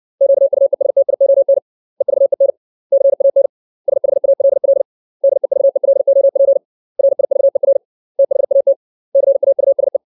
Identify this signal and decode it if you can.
Morse code, heard as a keyed tone, about 60 wpm, 550 Hz. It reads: QUESTION EVEN CAN SHALL DEVELOP RIVER THAT CAUSE
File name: 632 words / 632 Words 60WPM